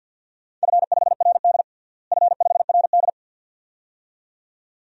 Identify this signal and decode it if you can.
Morse code, heard as a keyed tone, about 46 wpm, 700 Hz. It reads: F5RD F5RD
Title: DXCC-100 Call Signs - 4 Characters - Repeated 2x - 50wpm